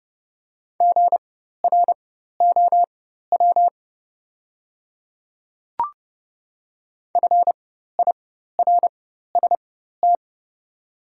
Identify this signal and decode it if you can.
Morse code, heard as a keyed tone, about 30 wpm, spaced as for 14 wpm, 700 Hz. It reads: GROW FIRST
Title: 4x Character Spacing - Top 300 Words - 30wpm